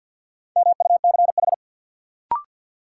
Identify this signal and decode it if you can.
Morse code, heard as a keyed tone, about 52 wpm, 700 Hz. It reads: MUCH
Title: Sets of 2 Words - New Word Drill - 50wpm